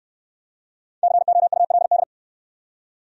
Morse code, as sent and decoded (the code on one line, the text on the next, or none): -.-. --- ..- .-.. -..
COULD